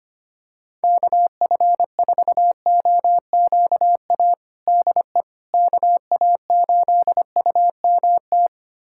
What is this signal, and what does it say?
Morse code, heard as a keyed tone, about 25 wpm, 700 Hz. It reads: KF4OQA DE KA8UMT
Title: USA Callsign DE Callsign - 6 Characters - 25wpm